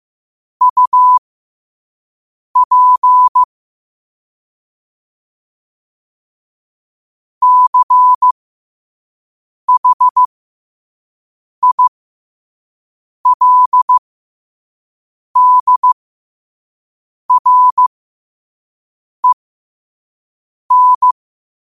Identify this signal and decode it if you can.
Morse code, heard as a keyed tone, about 15 wpm, spaced as for 5 wpm, 1000 Hz. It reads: UP CHILDREN
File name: qotdfiles / 5wpm